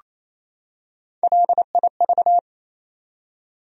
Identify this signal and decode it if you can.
Morse code, heard as a keyed tone, about 27 wpm, 700 Hz. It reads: LIV
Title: USA Common Names - Max 3 Letters - 28wpm